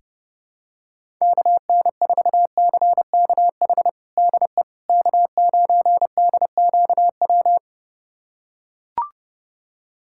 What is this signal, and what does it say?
Morse code, heard as a keyed tone, about 32 wpm, 700 Hz. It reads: KN4CKH DE K9DQW
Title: USA Callsign DE Callsign 30wpm